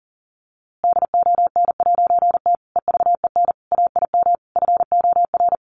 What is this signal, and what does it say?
Morse code, heard as a keyed tone, about 40 wpm, 700 Hz. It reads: DON'T EVEN AIM FOR
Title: A Sound Of Thunder 40wpm